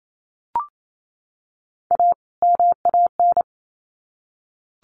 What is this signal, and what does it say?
Morse code, heard as a keyed tone, about 28 wpm, 700 Hz. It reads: A MAN
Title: Sets of 2 Words 28wpm